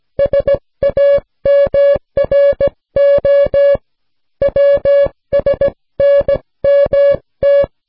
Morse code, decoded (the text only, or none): SAMRO WSNMT